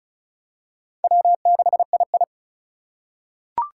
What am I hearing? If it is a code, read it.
Morse code, heard as a keyed tone, about 36 wpm, 700 Hz. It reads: W6II
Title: USA Call Signs - 4 Characters - 4x Character Spacing - 35wpm